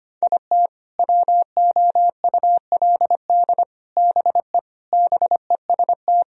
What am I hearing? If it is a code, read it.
Morse code, heard as a keyed tone, about 25 wpm, 700 Hz. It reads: IT WOULD BE BEST
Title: Sentences from Top 1000 Words - Review Part 2 - 25wpm